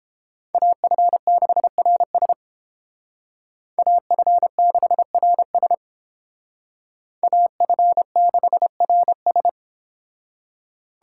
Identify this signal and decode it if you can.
Morse code, heard as a keyed tone, about 29 wpm, 700 Hz. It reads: AF6RS AF6RS AF6RS